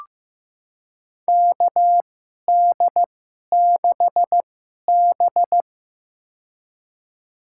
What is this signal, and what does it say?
Morse code, heard as a keyed tone, about 15 wpm, spaced as for 11 wpm, 700 Hz. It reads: KD6B
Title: USA Call Signs - 4 Characters - 2x Character Spacing - 15wpm